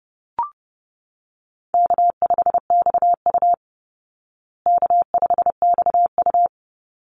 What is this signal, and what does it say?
Morse code, heard as a keyed tone, about 30 wpm, 700 Hz. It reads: K5XU K5XU
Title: Easy Call Signs Repeated 3x - 30wpm